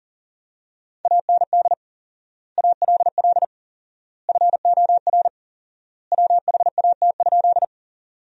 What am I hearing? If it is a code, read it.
Morse code, heard as a keyed tone, about 40 wpm, 700 Hz. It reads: AND ALL FOR WHAT?
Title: 4x Word-Spacing - Sentences from Top 100 Words - 40wpm